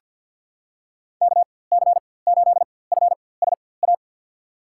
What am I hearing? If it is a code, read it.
Morse code, heard as a keyed tone, about 49 wpm, spaced as for 23 wpm, 700 Hz. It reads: KC8FSA
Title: USA Call Signs - 4x Character Spacing - 50wpm